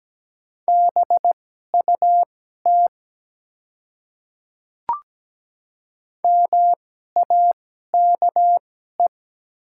Morse code, heard as a keyed tone, about 17 wpm, spaced as for 12 wpm, 700 Hz. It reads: BUT MAKE
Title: 2x Character Spacing - Top 200 Words - 17wpm